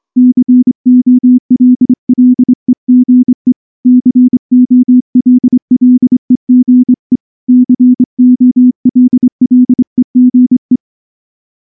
Morse code, heard as a keyed tone, about 24 wpm, 250 Hz. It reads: COLLEGE COLLEGE COLLEGE
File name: morse / morse-training-25WPM